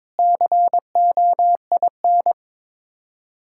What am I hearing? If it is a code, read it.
Morse code, heard as a keyed tone, about 22 wpm, 700 Hz. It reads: COIN
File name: practice / 22wpm / Sets of 7 Words - 22wpm